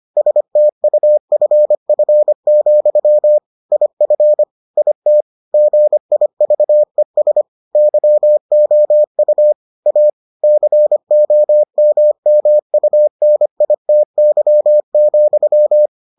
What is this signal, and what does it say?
Morse code, heard as a keyed tone, about 25 wpm, 600 Hz. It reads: STUFF, IF IT GIVES YOU A COMMUNITY,